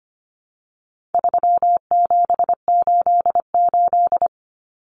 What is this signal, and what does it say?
Morse code, heard as a keyed tone, about 25 wpm, 700 Hz. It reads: 3788
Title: Common Cut Numbers - 4-Digit 25wpm